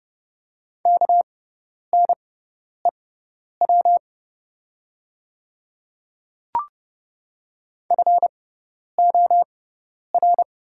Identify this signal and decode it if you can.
Morse code, heard as a keyed tone, about 30 wpm, spaced as for 10 wpm, 700 Hz. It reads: KNEW FOR
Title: 6x Character Spacing - Top 500 Words - 30wpm